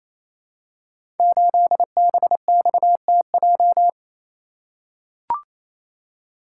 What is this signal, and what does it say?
Morse code, heard as a keyed tone, about 27 wpm, 700 Hz. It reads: 8BXTJ
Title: Code Groups - 5 Letter:Numbers - 28wpm